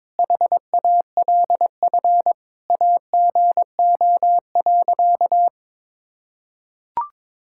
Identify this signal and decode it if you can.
Morse code, heard as a keyed tone, about 22 wpm, 700 Hz. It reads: HALF AGO.
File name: Sentences with 7-Word Phrases - 22wpm